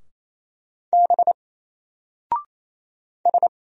Morse code, heard as a keyed tone, about 27 wpm, 700 Hz. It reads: B S
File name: Single-Letter-Number-28wpm